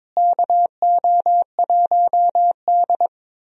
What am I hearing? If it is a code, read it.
Morse code, heard as a keyed tone, about 23 wpm, 700 Hz. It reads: KO1D
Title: USA Call Signs - 4 Characters - Repeated 3x - 22wpm